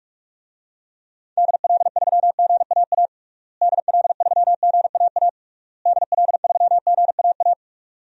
Morse code, heard as a keyed tone, about 41 wpm, 700 Hz. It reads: DL3GAA DL3GAA DL3GAA
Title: DXCC-100 Call Signs - 6 Characters - Repeated 3x - 45wpm